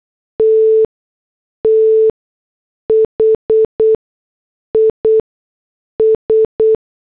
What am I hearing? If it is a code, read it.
Morse code, heard as a keyed tone, about 8 wpm, spaced as for 6 wpm, 440 Hz. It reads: T THIS
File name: morse / headlines